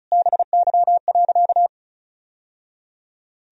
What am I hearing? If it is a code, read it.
Morse code, heard as a keyed tone, about 36 wpm, 700 Hz. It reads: BY.
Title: Sentences from Top 1000 Words - Review Part 1 - 35wpm